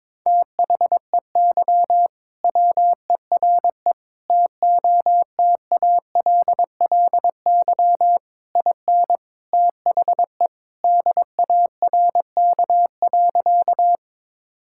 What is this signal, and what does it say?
Morse code, heard as a keyed tone, about 22 wpm, 700 Hz. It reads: THEY WERE TOTALLY IN THE DARK.
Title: Sentences from Top 1600 Words 22wpm